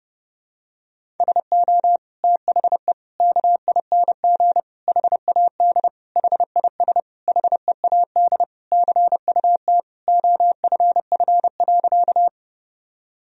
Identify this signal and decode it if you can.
Morse code, heard as a keyed tone, about 30 wpm, 700 Hz. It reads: SO THE KING HAD HIS HEAD CUT OFF.
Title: Sentences from Top 1000 Words - Review Part 1 - 30wpm